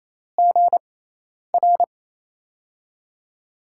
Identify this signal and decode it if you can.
Morse code, heard as a keyed tone, about 28 wpm, spaced as for 10 wpm, 700 Hz. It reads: GR
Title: USA Call Signs - 6x Character Spacing - 28wpm